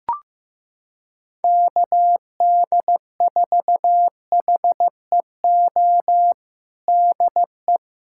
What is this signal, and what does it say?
Morse code, heard as a keyed tone, about 15 wpm, 700 Hz. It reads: KD4HEO DE
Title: USA Callsign DE Callsign 15wpm